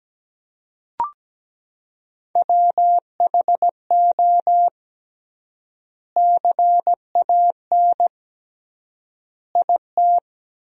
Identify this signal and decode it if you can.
Morse code, heard as a keyed tone, about 17 wpm, 700 Hz. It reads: WHO CAN IT
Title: 3x Word-Spacing - Sentences from Top 100 Words - 17wpm